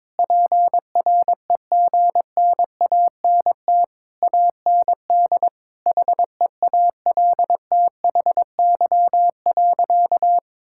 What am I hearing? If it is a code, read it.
Morse code, heard as a keyed tone, about 22 wpm, 700 Hz. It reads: PREGNANT AND HEALTHY.
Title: Sentences from Top 1000 Words - Review Part 6 - 22wpm